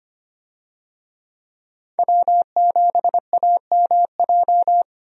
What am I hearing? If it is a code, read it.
Morse code, heard as a keyed tone, about 25 wpm, 700 Hz. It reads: W7AMJ
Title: USA Call Signs - 5 Characters - 25wpm